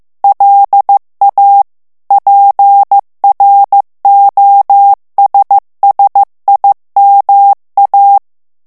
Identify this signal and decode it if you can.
Morse code, heard as a keyed tone, about 15 wpm, 800 Hz. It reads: LA PROSSIMA
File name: ITALIEN texte clair 1 vitesse 14 mots minute